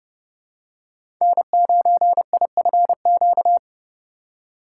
Morse code, heard as a keyed tone, about 30 wpm, 700 Hz. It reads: N9IFQ
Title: USA Call Signs - 5 Characters - 2x Character Spacing - 30wpm